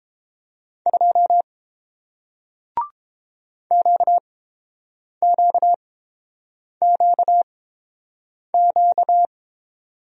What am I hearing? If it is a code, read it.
Morse code, heard as a keyed tone, about 30 wpm, spaced as for 7 wpm, 700 Hz. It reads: 2 QQQQ